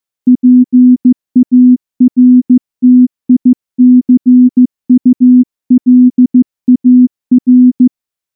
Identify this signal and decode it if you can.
Morse code, heard as a keyed tone, about 15 wpm, 250 Hz. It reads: PARTICULAR
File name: morse / morse-training-15WPM